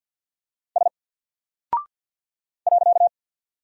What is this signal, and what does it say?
Morse code, heard as a keyed tone, about 48 wpm, 700 Hz. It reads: S .
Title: Single Character (Letter:Number:Punctuation:Prosign) 50wpm